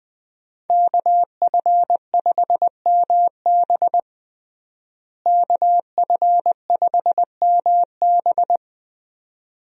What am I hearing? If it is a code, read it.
Morse code, heard as a keyed tone, about 20 wpm, 700 Hz. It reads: KF5MB KF5MB
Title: USA Call Signs - 5 Characters - Repeated 2x - 20wpm